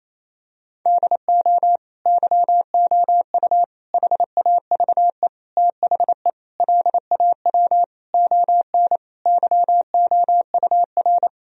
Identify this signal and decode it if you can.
Morse code, heard as a keyed tone, about 28 wpm, 700 Hz. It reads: DO YOU HAVE THE LAW ON YOUR
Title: Sentences from Top 1000 Words - Review Part 3 - 28wpm